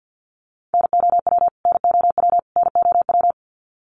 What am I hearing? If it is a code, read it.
Morse code, heard as a keyed tone, about 50 wpm, 700 Hz. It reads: NOW NOW NOW
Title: Top 500 Words Repeated 50wpm